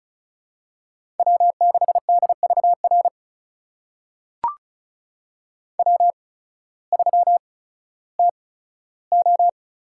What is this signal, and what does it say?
Morse code, heard as a keyed tone, about 36 wpm, spaced as for 16 wpm, 700 Hz. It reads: W6DVR W3TO
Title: USA Call Signs - 5 Characters - 8x Character Spacing - 35wpm